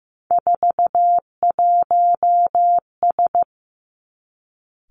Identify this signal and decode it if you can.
Morse code, heard as a keyed tone, about 15 wpm, 700 Hz. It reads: 41S
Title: Three Challenging Letter-Number Combinations 15wpm